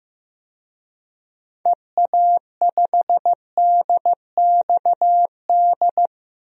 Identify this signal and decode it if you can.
Morse code, heard as a keyed tone, about 16 wpm, 700 Hz. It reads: EA5DXD
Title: DXCC-100 Call Signs - 6 Characters - Repeated 3x - 15wpm